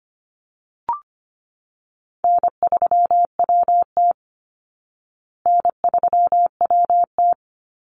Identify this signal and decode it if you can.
Morse code, heard as a keyed tone, about 25 wpm, 700 Hz. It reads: N3WT N3WT
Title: Easy Call Signs Repeated 3x - 25wpm